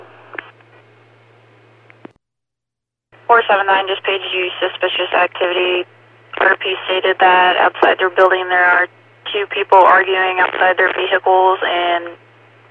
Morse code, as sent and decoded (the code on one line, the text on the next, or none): none